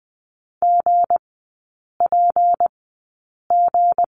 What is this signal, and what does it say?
Morse code, heard as a keyed tone, about 20 wpm, spaced as for 8 wpm, 690 Hz. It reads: GPG
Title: 159 Lesson 26e - Letter g - Three Characters 20wpm